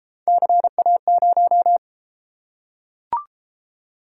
Morse code, heard as a keyed tone, about 31 wpm, 700 Hz. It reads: CA0